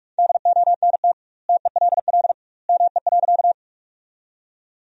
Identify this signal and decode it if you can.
Morse code, heard as a keyed tone, about 45 wpm, 700 Hz. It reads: DONT TELL ME.
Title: 2x Word-Spacing - Sentences from Top 300 Words - 45wpm